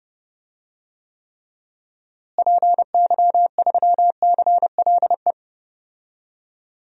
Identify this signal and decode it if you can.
Morse code, heard as a keyed tone, about 30 wpm, 700 Hz. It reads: PY3CLE
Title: DXCC-100 Call Signs - 6 Characters - Repeated 2x - 30wpm